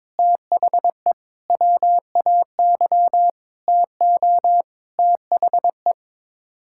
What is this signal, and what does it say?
Morse code, heard as a keyed tone, about 22 wpm, 700 Hz. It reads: THE WAY TO THE
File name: Sets of 9 Words - 22wpm